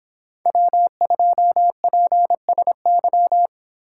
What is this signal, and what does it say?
Morse code, heard as a keyed tone, about 26 wpm, 700 Hz. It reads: W2PSY
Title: Speed Racing - USA 1x3 Call Signs 22wpm